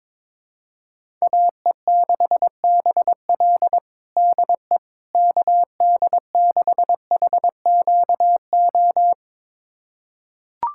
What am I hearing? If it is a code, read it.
Morse code, heard as a keyed tone, about 22 wpm, 700 Hz. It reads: AE6BL DE KD6HQO
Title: USA Callsign DE Callsign 22wpm